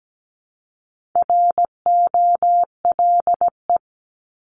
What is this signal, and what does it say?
Morse code, heard as a keyed tone, about 17 wpm, 700 Hz. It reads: ROLE
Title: Top 751-1000 Words 17wpm